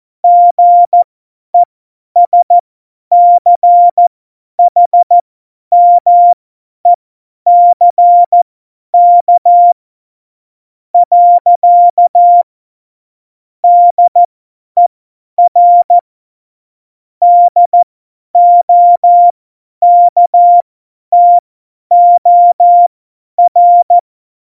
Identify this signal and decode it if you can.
Morse code, heard as a keyed tone, about 14 wpm, spaced as for 10 wpm, 700 Hz. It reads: GESCHMECK . DER DOKTOR
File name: Morsekurs / Witze / Witz 13. Oktober 2020, 50 F70